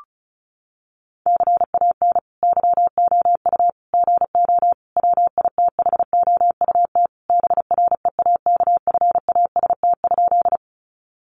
Morse code, heard as a keyed tone, about 34 wpm, 700 Hz. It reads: CAN YOU GO WITHOUT BREAKFAST?
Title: Sentences from Top 1500 Words 35wpm